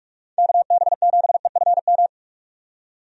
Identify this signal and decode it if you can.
Morse code, heard as a keyed tone, about 45 wpm, 700 Hz. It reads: KB7EFM
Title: USA Call Signs - 6 Characters - 45wpm